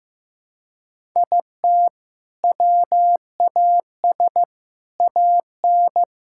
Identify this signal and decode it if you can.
Morse code, heard as a keyed tone, about 15 wpm, 700 Hz. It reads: IT WAS AN
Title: Sentences from Top 1000 Words - Review Part 5 - 15wpm